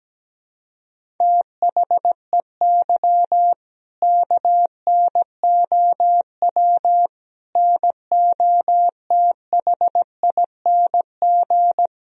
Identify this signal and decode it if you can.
Morse code, heard as a keyed tone, about 17 wpm, 700 Hz. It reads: THEY KNOW NOTHING
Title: Sentences from Top 500 Words and Abbreviations 17wpm